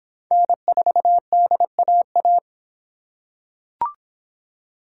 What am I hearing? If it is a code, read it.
Morse code, heard as a keyed tone, about 23 wpm, 700 Hz. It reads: N4DAA